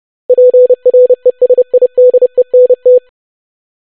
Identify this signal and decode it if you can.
Morse code, heard as a keyed tone, about 30 wpm, 500 Hz. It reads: PRESIDENT
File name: basic500-25wpm